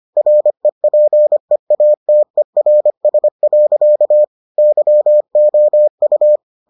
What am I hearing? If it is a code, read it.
Morse code, heard as a keyed tone, about 25 wpm, 600 Hz. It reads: REPEATERS. YOU